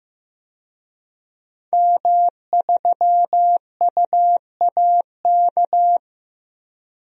Morse code, heard as a keyed tone, about 15 wpm, 700 Hz. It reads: M3UAK
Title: DXCC-100 Call Signs - 5 Characters - Repeated 2x - 15wpm